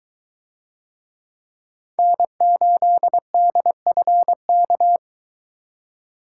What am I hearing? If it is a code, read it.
Morse code, heard as a keyed tone, about 23 wpm, 700 Hz. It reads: N8DFK